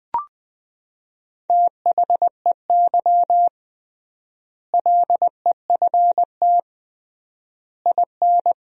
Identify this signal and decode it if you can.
Morse code, heard as a keyed tone, about 20 wpm, 700 Hz. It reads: THEY LEFT IN
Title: 3x Word-Spacing - Sentences from Top 400 Words - 20wpm